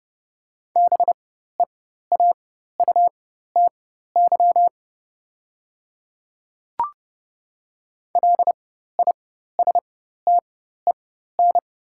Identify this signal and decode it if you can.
Morse code, heard as a keyed tone, about 32 wpm, spaced as for 14 wpm, 700 Hz. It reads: BEAUTY LISTEN
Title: 4x Character Spacing - Top 500 Words - 30wpm